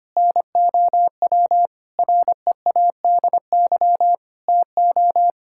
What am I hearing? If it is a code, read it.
Morse code, heard as a keyed tone, about 25 wpm, 700 Hz. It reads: NOW READY TO
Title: Sentences from Top 1600 Words 25wpm